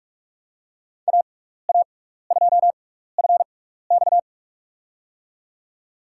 Morse code, heard as a keyed tone, about 46 wpm, spaced as for 15 wpm, 700 Hz. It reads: AA2FX
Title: USA Call Signs - 5 Characters - 6x Character Spacing - 45wpm